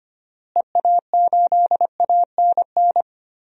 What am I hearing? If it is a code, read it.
Morse code, heard as a keyed tone, about 25 wpm, 700 Hz. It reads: EA8ANN
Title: DXCC-100 Call Signs - 6 Characters - Repeated 2x - 25wpm